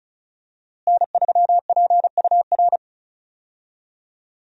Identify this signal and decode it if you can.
Morse code, heard as a keyed tone, about 36 wpm, 700 Hz. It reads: N3PUR
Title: USA Call Signs - 5 Characters - Repeated 3x - 35wpm